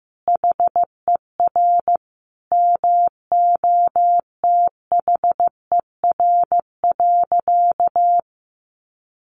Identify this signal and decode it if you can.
Morse code, heard as a keyed tone, about 15 wpm, 700 Hz. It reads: HER MOTHER.
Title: Sentences from Top 800 Words 15wpm